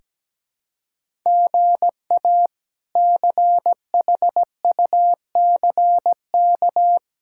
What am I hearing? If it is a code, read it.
Morse code, heard as a keyed tone, about 17 wpm, 700 Hz. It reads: GA CHUCK
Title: K1USN SST QSO Practice - 17wpm